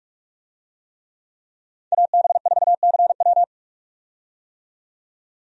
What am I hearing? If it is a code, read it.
Morse code, heard as a keyed tone, about 42 wpm, 700 Hz. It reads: AB4CW
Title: USA Call Signs - 5 Characters - 45wpm